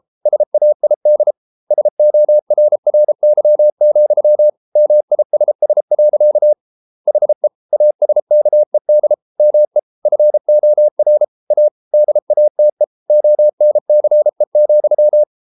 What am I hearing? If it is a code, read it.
Morse code, heard as a keyed tone, about 33 wpm, 600 Hz. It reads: SAID SORRY, MISS. HE ASKED ME FOR A DATE ONCE,